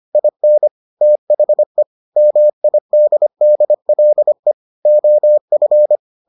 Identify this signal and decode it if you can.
Morse code, heard as a keyed tone, about 25 wpm, 600 Hz. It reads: IN THE MIDDLE OF